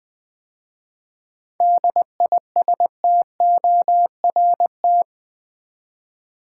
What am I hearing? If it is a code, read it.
Morse code, heard as a keyed tone, about 20 wpm, 700 Hz. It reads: DISTORT
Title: 7 Letter Words - First 3 Letters Given - 20wpm